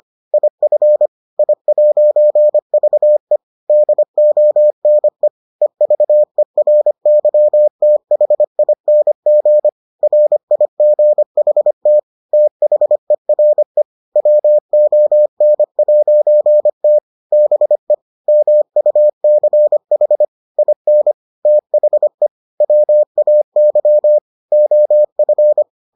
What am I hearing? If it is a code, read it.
Morse code, heard as a keyed tone, about 25 wpm, 600 Hz. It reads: IF I'VE DONE EVERYTHING RIGHT THERE WON'T BE MUCH IN THE WAY OF